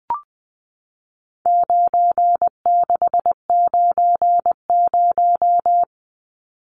Morse code, heard as a keyed tone, about 20 wpm, 700 Hz. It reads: 9690
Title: Numbers - 4-Digit 20wpm